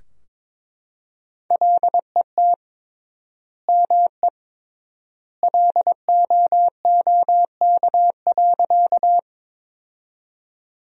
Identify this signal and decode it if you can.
Morse code, heard as a keyed tone, about 21 wpm, 700 Hz. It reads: LET ME LOOK.
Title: 3x Word-Spacing - Sentences from Top 300 Words - 22wpm